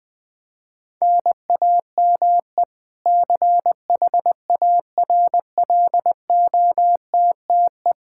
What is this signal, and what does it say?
Morse code, heard as a keyed tone, about 20 wpm, 700 Hz. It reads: NAME CHARLOTTE
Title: Practice QSO - Exchange 1 - Beginning - 20wpm